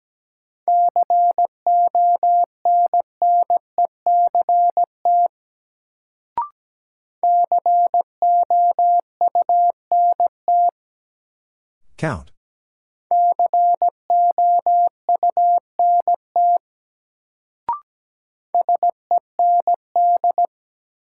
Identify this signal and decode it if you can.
Morse code, heard as a keyed tone, about 17 wpm, 700 Hz. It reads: CONNECT COUNT COUNT SEND
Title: Sentences from Top 1500 Words - New Word Drill - 17wpm